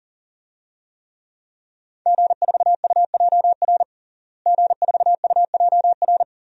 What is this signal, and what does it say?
Morse code, heard as a keyed tone, about 40 wpm, 700 Hz. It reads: G4UJR G4UJR
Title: DXCC-100 Call Signs - 5 Characters - Repeated 2x - 40wpm